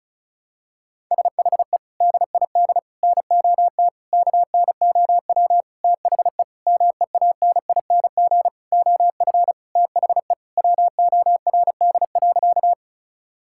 Morse code, heard as a keyed tone, about 36 wpm, 700 Hz. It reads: SHE DID NOT KNOW THE MEANING OF THE WORD.
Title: Sentences from Top 1000 Words - Review Part 6 - 35wpm